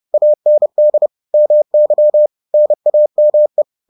Morse code, heard as a keyed tone, about 30 wpm, 600 Hz. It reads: AND MY NAME